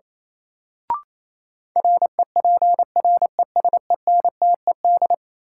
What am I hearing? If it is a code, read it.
Morse code, heard as a keyed tone, about 28 wpm, 700 Hz. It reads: REPRESENTED
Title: Sets of 5 Words - New Word Drill - 28wpm